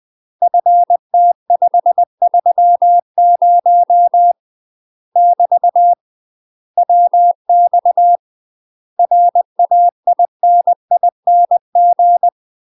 Morse code, heard as a keyed tone, about 20 wpm, 700 Hz. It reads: FT530 <BT> WX RAINING
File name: UK MASTER QSOs TRAINING / 20WPM / UK MASTER QSOs 0052